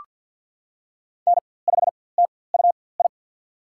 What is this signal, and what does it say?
Morse code, heard as a keyed tone, about 47 wpm, spaced as for 23 wpm, 700 Hz. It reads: N5TUI
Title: USA Call Signs - 5 Characters - 4x Character Spacing - 50wpm